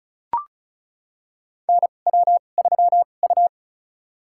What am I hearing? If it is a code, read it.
Morse code, heard as a keyed tone, about 36 wpm, spaced as for 25 wpm, 700 Hz. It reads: NW3U
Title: USA Call Signs - 4 Characters - 2x Character Spacing - 35wpm